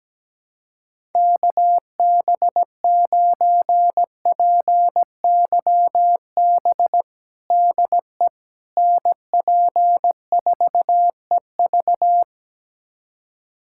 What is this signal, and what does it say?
Morse code, heard as a keyed tone, about 17 wpm, 700 Hz. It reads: KB9PYB DE NP4EV
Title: USA Callsign DE Callsign 17wpm